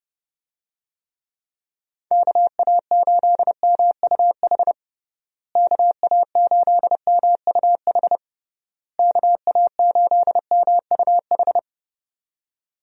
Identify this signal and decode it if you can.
Morse code, heard as a keyed tone, about 32 wpm, 700 Hz. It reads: KA8MUH KA8MUH KA8MUH
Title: USA Call Signs - 6 Characters - Repeated 3x - 30wpm